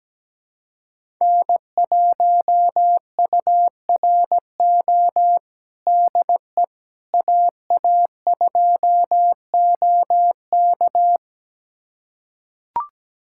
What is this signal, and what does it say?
Morse code, heard as a keyed tone, about 17 wpm, 700 Hz. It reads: N1URO DE AA2OK
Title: USA Callsign DE Callsign - 5 Characters - 17wpm